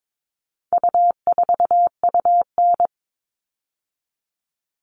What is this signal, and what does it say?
Morse code, heard as a keyed tone, about 22 wpm, 700 Hz. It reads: U4UN
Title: All Cut Numbers - 4-Digit 22wpm